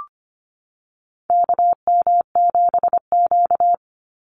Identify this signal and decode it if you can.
Morse code, heard as a keyed tone, about 25 wpm, 700 Hz. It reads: KM7Q
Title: Call Signs - 25wpm